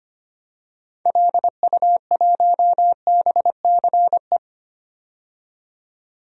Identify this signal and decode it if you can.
Morse code, heard as a keyed tone, about 25 wpm, 700 Hz. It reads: LU1BCE
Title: DXCC-100 Call Signs - 6 Characters - 25wpm